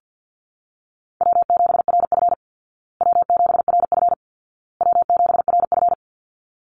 Morse code, heard as a keyed tone, about 50 wpm, 700 Hz. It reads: W7RF W7RF W7RF
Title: Easy Call Signs Repeated 3x - 50wpm